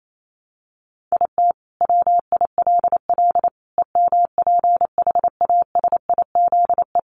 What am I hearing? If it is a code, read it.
Morse code, heard as a keyed tone, about 28 wpm, 700 Hz. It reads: IT WILL EMPHASIZE
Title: Sentences from Top 1500 Words 28wpm